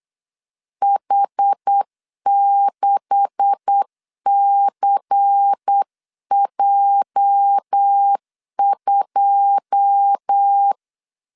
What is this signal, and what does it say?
Morse code, heard as a keyed tone, about 8 wpm, 800 Hz. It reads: H6CJ2
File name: CW / DET / 1 9 1500C S C 8wpm 5653